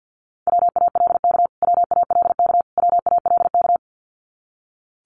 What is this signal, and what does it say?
Morse code, heard as a keyed tone, about 50 wpm, 700 Hz. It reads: WALK WALK WALK
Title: Top 500 Words Repeated 50wpm